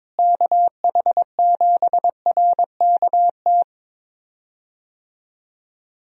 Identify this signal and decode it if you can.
Morse code, heard as a keyed tone, about 22 wpm, 700 Hz. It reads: KH7RKT
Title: USA Call Signs - 6 Characters - Repeated 3x - 22wpm